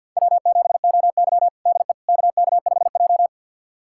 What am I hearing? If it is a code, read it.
Morse code, heard as a keyed tone, about 48 wpm, 700 Hz. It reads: W7OY DE KK5J
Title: USA Callsign DE Callsign - 4 Characters - 50wpm